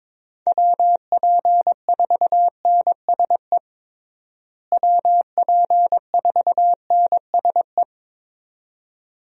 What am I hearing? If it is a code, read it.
Morse code, heard as a keyed tone, about 22 wpm, 700 Hz. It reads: WP4NSE WP4NSE
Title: USA Call Signs - 6 Characters - Repeated 2x - 22wpm